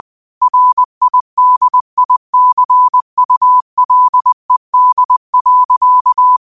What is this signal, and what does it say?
Morse code, heard as a keyed tone, about 20 wpm, 1000 Hz. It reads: RIDICULED.
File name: qotdfiles / 20wpm